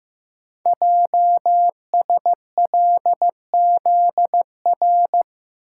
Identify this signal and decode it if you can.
Morse code, heard as a keyed tone, about 15 wpm, 700 Hz. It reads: JSLZR
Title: Code Groups - 5 Letters - 15wpm